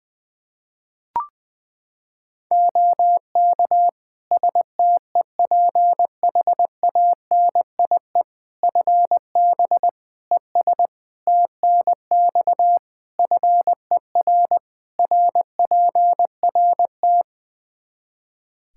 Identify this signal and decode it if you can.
Morse code, heard as a keyed tone, about 20 wpm, 700 Hz. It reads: OK STEPHANIE FB ES TNX FER RPRT
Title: Practice QSO - Exchange 2 - Info - 20wpm